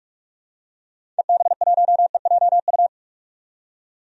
Morse code, heard as a keyed tone, about 46 wpm, 700 Hz. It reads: EB1EJU
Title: DXCC-100 Call Signs - 6 Characters - 45wpm